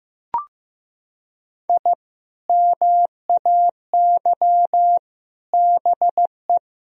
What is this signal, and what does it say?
Morse code, heard as a keyed tone, about 16 wpm, 700 Hz. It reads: I MAY BE
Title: Sentences from Top 1000 Words - Review Part 1 - 15wpm